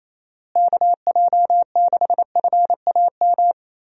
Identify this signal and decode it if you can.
Morse code, heard as a keyed tone, about 27 wpm, 700 Hz. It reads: KJ6FAM
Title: USA Call Signs - 6 Characters - Repeated 2x - 28wpm